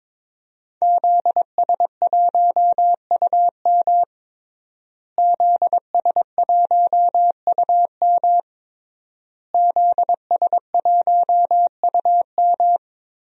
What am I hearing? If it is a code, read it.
Morse code, heard as a keyed tone, about 22 wpm, 700 Hz. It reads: ZS1UM ZS1UM ZS1UM
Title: DXCC-100 Call Signs - 5 Characters - Repeated 3x - 22wpm